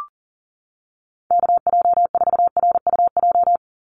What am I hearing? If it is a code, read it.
Morse code, heard as a keyed tone, about 40 wpm, 700 Hz. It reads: KJ4RUJ
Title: USA Call Signs 40wpm